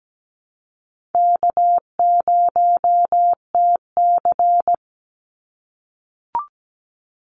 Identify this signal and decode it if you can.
Morse code, heard as a keyed tone, about 17 wpm, 700 Hz. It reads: K0TC
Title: Easy Call Signs - 17wpm